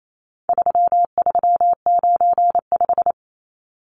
Morse code, heard as a keyed tone, about 28 wpm, 700 Hz. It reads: 3395
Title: Numbers - 4-Digit 28wpm